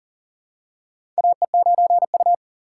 Morse code, heard as a keyed tone, about 40 wpm, 700 Hz. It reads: AE9U
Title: USA Call Signs - 4x Character Spacing - 40wpm